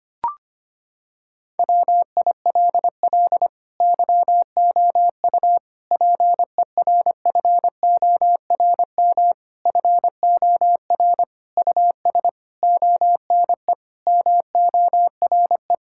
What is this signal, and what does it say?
Morse code, heard as a keyed tone, about 25 wpm, 700 Hz. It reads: WILL YOU PERFORM FOR US ONE MORE
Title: Sentences from Top 1000 Words - Review Part 5 - 25wpm